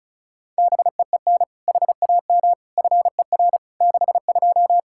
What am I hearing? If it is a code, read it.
Morse code, heard as a keyed tone, about 36 wpm, 700 Hz. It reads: BEEN HAM FER 62
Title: K1USN SST QSO Practice 35wpm